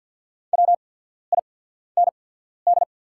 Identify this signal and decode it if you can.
Morse code, heard as a keyed tone, about 48 wpm, spaced as for 14 wpm, 690 Hz. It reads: WIND
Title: 8x Character Spacing - Top 500 Words - 50wpm